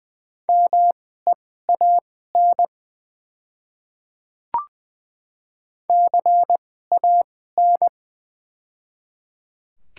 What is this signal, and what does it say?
Morse code, heard as a keyed tone, about 20 wpm, spaced as for 14 wpm, 700 Hz. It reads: MEAN CAN
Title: 2x Character Spacing - Top 200 Words - 20wpm